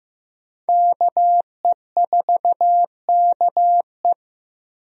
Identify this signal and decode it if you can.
Morse code, heard as a keyed tone, about 15 wpm, 700 Hz. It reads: KE4KE
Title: USA Call Signs - 5 Characters - Repeated 2x - 15wpm